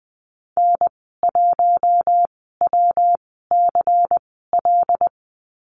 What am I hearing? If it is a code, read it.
Morse code, heard as a keyed tone, about 20 wpm, spaced as for 14 wpm, 700 Hz. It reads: N1WCL
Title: 122 Lesson 20g - Number 3 - Call Signs 20wpm